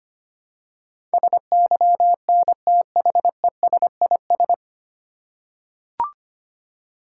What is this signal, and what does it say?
Morse code, heard as a keyed tone, about 25 wpm, 700 Hz. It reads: SYNTHESIS
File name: practice / 25wpm / 9 Letter Words - First 3 Letters Given - 25wpm